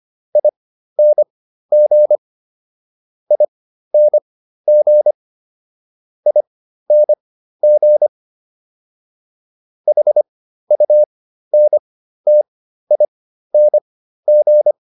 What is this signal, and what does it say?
Morse code, heard as a keyed tone, about 25 wpm, spaced as for 13 wpm, 600 Hz. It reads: ING ING ING HUNTING